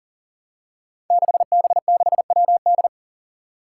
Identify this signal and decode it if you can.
Morse code, heard as a keyed tone, about 43 wpm, 700 Hz. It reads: 6B6WD
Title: Five Challenging Letter-Number Combinations 40wpm